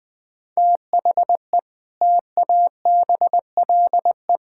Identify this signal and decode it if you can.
Morse code, heard as a keyed tone, about 20 wpm, 700 Hz. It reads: THE TABLE
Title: Sentences with 4-Word Phrases - 20wpm